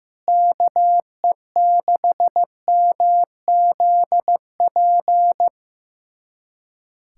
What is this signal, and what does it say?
Morse code, heard as a keyed tone, about 15 wpm, 700 Hz. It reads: KE6MZP
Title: USA Callsign DE Callsign - 6 Characters - 15wpm